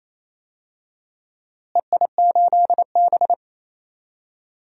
Morse code, heard as a keyed tone, about 28 wpm, 700 Hz. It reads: EI8B
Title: DXCC-100 Call Signs - 4 Characters - Repeated 3x - 28wpm